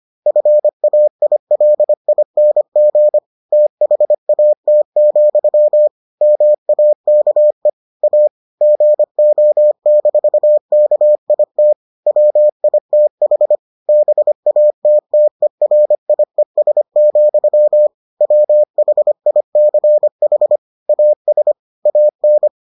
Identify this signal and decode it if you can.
Morse code, heard as a keyed tone, about 25 wpm, 600 Hz. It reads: FAILING THAT, MAKE A GO-KIT WITH BATTERIES, WHICH AS AN